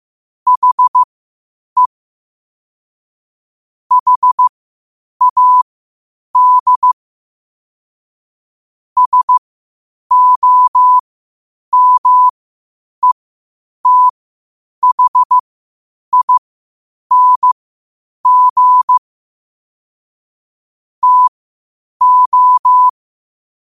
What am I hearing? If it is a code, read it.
Morse code, heard as a keyed tone, about 15 wpm, spaced as for 8 wpm, 1000 Hz. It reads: HE HAD SOMETHING TO
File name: qotdfiles / 8wpm